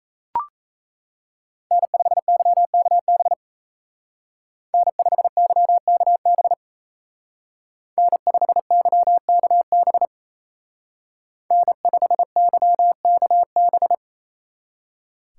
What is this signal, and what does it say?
Morse code, heard as a keyed tone, about 32 wpm, 700 Hz. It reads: N5YKB N5YKB N5YKB N5YKB
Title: Speed Racing - USA Call Signs 28wpm